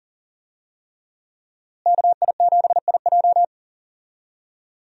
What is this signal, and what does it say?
Morse code, heard as a keyed tone, about 40 wpm, 700 Hz. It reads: KI7IJ
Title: USA Call Signs - 5 Characters - Repeated 3x - 40wpm